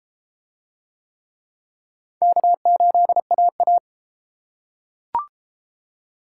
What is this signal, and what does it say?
Morse code, heard as a keyed tone, about 30 wpm, 700 Hz. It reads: K8AA